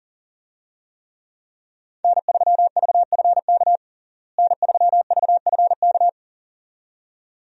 Morse code, heard as a keyed tone, about 40 wpm, 700 Hz. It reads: N3VFK N3VFK
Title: USA Call Signs - 5 Characters - Repeated 2x - 40wpm